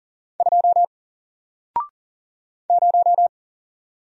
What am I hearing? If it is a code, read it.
Morse code, heard as a keyed tone, about 40 wpm, 700 Hz. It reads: 2 0
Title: Single Letter-Number Alt 40wpm